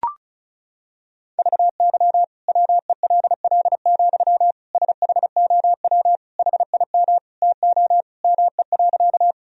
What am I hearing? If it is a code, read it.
Morse code, heard as a keyed tone, about 36 wpm, 690 Hz. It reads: VY WELL, SHOW HIM TO ME.
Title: Sentences from Top 200 Words and Abbreviations 35wpm